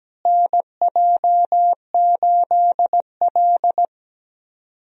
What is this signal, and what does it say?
Morse code, heard as a keyed tone, about 17 wpm, 700 Hz. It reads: NJ8L
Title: USA Call Signs - 4 Characters - Repeated 3x - 17wpm